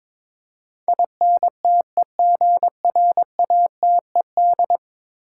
Morse code, heard as a keyed tone, about 22 wpm, 700 Hz. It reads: INTEGRATED
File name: practice / 22wpm / 10 Letter Words - First 3 Letters Given - 22wpm